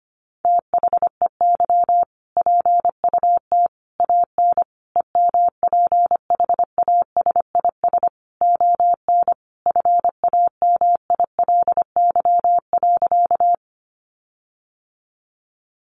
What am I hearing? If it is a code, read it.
Morse code, heard as a keyed tone, about 25 wpm, 700 Hz. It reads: THEY PUT AN EMPHASIS ON FAMILY.
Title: Sentences from Top 1400 Words 25wpm